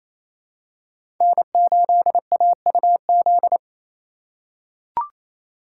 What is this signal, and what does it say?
Morse code, heard as a keyed tone, about 28 wpm, 700 Hz. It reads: N8AUZ
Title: USA Call Signs - 5 Characters - 4x Character Spacing - 28wpm